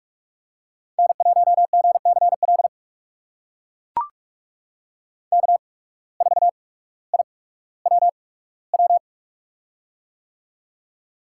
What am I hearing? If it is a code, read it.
Morse code, heard as a keyed tone, about 44 wpm, spaced as for 19 wpm, 700 Hz. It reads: N1GCL K4IWW
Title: USA Call Signs - 5 Characters - 8x Character Spacing - 45wpm